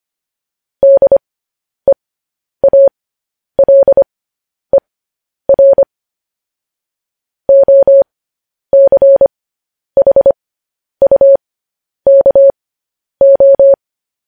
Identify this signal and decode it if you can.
Morse code, heard as a keyed tone, about 25 wpm, spaced as for 10 wpm, 550 Hz. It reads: DEALER OCHUKO
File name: podcast / News headlines-1x-25x10